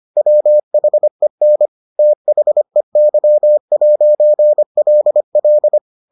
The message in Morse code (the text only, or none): WHEN THEY'LL